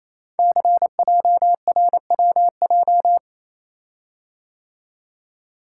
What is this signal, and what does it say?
Morse code, heard as a keyed tone, about 27 wpm, 700 Hz. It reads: CJRWJ
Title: Code Groups - 5 Letters - 28wpm